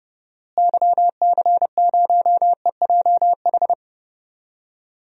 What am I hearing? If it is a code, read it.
Morse code, heard as a keyed tone, about 30 wpm, 700 Hz. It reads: YC0EJH
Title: DXCC-100 Call Signs - 6 Characters - Repeated 3x - 30wpm